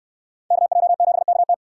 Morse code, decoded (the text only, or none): KJ7KT